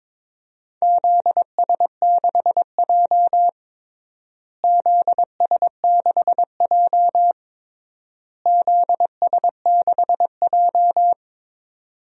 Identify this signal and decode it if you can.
Morse code, heard as a keyed tone, about 22 wpm, 700 Hz. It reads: ZS6J ZS6J ZS6J
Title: DXCC-100 Call Signs - 4 Characters - Repeated 3x - 22wpm